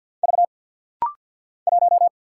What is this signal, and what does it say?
Morse code, heard as a keyed tone, about 50 wpm, 700 Hz. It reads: V 1
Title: Single Letter-Number - Rapid-Fire 50wpm